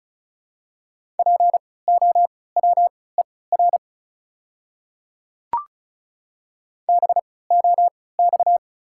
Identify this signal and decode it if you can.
Morse code, heard as a keyed tone, about 36 wpm, spaced as for 20 wpm, 700 Hz. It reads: POWER BOX
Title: 3x Character Spacing - Top 500 Words - 35wpm